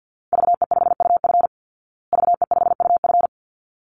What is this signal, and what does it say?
Morse code, heard as a keyed tone, about 49 wpm, 700 Hz. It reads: VE5UF VE5UF
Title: Call Signs Repeated 3x - 50wpm